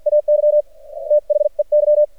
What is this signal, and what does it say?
Morse code, heard as a keyed tone, about 55 wpm, spaced as for 44 wpm, 600 Hz. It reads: AY THEY
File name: archived / CRSnet